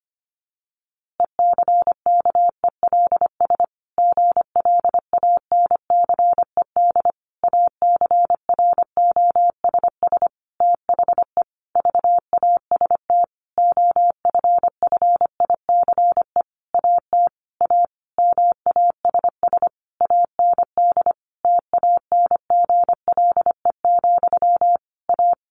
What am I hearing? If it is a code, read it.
Morse code, heard as a keyed tone, about 25 wpm, 700 Hz. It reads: ECKELS GLANCED ACROSS THE VAST OFFICE AT A MASS AND TANGLE, A
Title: A Sound Of Thunder 25wpm